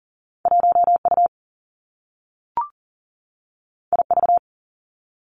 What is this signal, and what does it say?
Morse code, heard as a keyed tone, about 39 wpm, 700 Hz. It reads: 1 U I V
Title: Two Challenging Letter-Number Combinations 40wpm